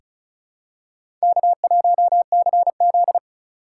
Morse code, heard as a keyed tone, about 36 wpm, 700 Hz. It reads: K1CZ
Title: USA Call Signs - 3x Character Spacing - 35wpm